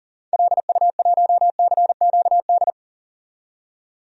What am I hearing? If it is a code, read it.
Morse code, heard as a keyed tone, about 40 wpm, 700 Hz. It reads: LU1CQD
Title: DXCC-100 Call Signs - 6 Characters - 40wpm